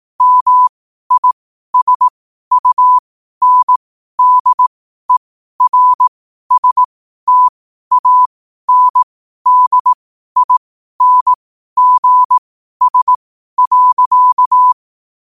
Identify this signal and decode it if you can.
Morse code, heard as a keyed tone, about 18 wpm, spaced as for 12 wpm, 1000 Hz. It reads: MISUNDERSTANDINGS.
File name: qotdfiles / 12wpm